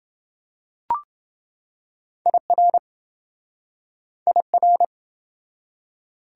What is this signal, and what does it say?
Morse code, heard as a keyed tone, about 28 wpm, 700 Hz. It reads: IR IR